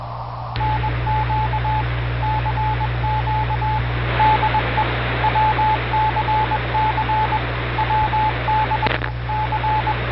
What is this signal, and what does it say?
Morse code, heard as a keyed tone, about 21 wpm, 870 Hz, with radio noise. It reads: NQ CQ DE WCC WD C